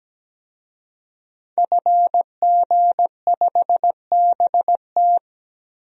Morse code, heard as a keyed tone, about 17 wpm, 700 Hz. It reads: FG5BT
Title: DXCC-100 Call Signs - 5 Characters - 17wpm